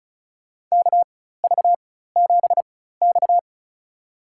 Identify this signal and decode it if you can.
Morse code, heard as a keyed tone, about 36 wpm, spaced as for 16 wpm, 700 Hz. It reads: KV7X
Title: USA Call Signs - 4x Character Spacing - 35wpm